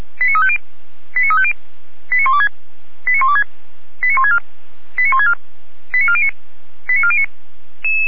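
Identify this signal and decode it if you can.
Five-tone selective call (ZVEI), heard as quick runs of five tones, tones of about 70 ms. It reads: 89350 89350 89126 89126 89164 89164 89409 89409
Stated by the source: ZVEI five-tone selective call received on SDR.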